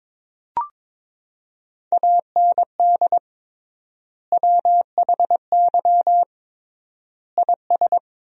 Morse code, heard as a keyed tone, about 22 wpm, 700 Hz. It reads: AND WHY IS
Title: 3x Word-Spacing - Sentences from Top 200 Words - 22wpm